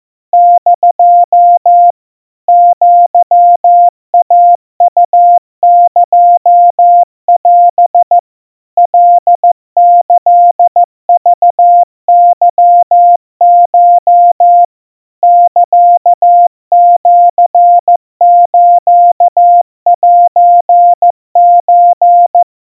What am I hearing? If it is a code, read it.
Morse code (Japanese Wabun code), heard as a keyed tone, about 14 wpm, 700 Hz. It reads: <DO> アイウエオ カキクケコ サシスセソ
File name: kawa / Morse sample / aiueo 50 cpm 700 Hz Crispy